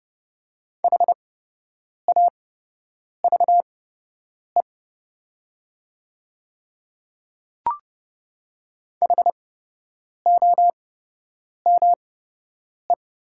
Morse code, heard as a keyed tone, about 30 wpm, spaced as for 8 wpm, 700 Hz. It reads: HAVE HOME
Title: 8x Character Spacing - Top 200 Words - 30wpm